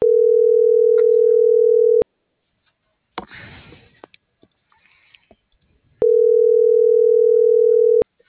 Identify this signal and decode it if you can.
Ringback tone (North America), 440 + 480 Hz.